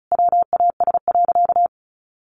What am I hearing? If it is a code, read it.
Morse code, heard as a keyed tone, about 34 wpm, 700 Hz. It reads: WAS.
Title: Sentences from Top 1200 Words 35wpm